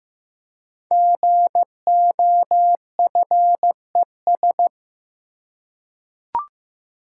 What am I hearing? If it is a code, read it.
Morse code, heard as a keyed tone, about 16 wpm, 700 Hz. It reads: GOFES
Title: Code Groups - 5 Letters - 15wpm